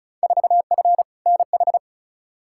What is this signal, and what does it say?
Morse code, heard as a keyed tone, about 36 wpm, 700 Hz. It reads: 4F NH
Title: Field Day Exchanges Encore - 35wpm